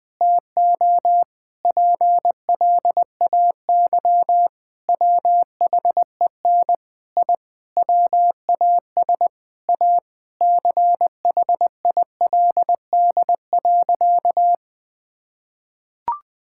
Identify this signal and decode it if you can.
Morse code, heard as a keyed tone, about 20 wpm, 700 Hz. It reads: TO PLAY WHEN I WAS A CHILD.
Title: Sentences with 4-Word Phrases - Spoken Beginning - 20wpm